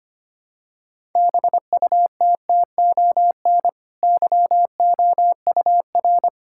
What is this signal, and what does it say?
Morse code, heard as a keyed tone, about 25 wpm, 700 Hz. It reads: BUTTON YOUR
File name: Easy Sentences with Hints - 25wpm